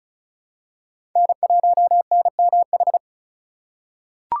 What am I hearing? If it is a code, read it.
Morse code, heard as a keyed tone, about 34 wpm, 700 Hz. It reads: N1NMH
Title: USA Call Signs - 5 Characters - 4x Character Spacing - 35wpm